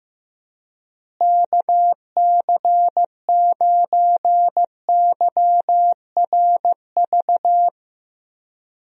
Morse code, heard as a keyed tone, about 16 wpm, 700 Hz. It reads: KC9YRV
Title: USA Call Signs - 6 Characters - Repeated 2x - 15wpm